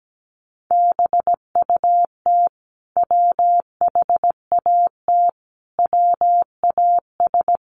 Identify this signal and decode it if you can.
Morse code, heard as a keyed tone, about 17 wpm, 700 Hz. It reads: BUT WHAT WAS
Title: Sentences from Top 1100 Words 17wpm